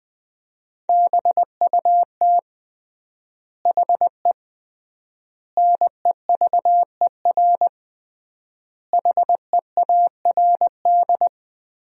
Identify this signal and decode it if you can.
Morse code, heard as a keyed tone, about 21 wpm, 700 Hz. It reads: BUT HE NEVER HEARD
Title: 3x Word-Spacing - Sentences from Top 400 Words - 20wpm